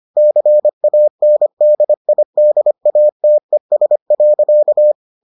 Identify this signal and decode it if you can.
Morse code, heard as a keyed tone, about 25 wpm, 600 Hz. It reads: CANDIDATES.